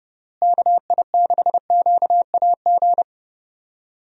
Morse code, heard as a keyed tone, about 30 wpm, 700 Hz. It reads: KI6QAG
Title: USA Call Signs - 6 Characters - Repeated 3x - 30wpm